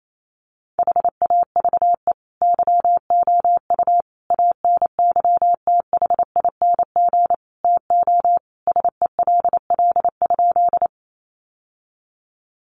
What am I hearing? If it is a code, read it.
Morse code, heard as a keyed tone, about 28 wpm, 700 Hz. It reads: HAVE YOU ANYTHING TO SELL?
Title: Sentences from Top 1100 Words 28wpm